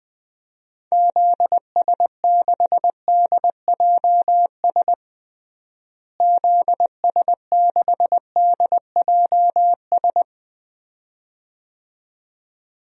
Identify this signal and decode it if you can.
Morse code, heard as a keyed tone, about 20 wpm, 700 Hz. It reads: ZS6DJS ZS6DJS
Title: DXCC-100 Call Signs - 6 Characters - Repeated 2x - 20wpm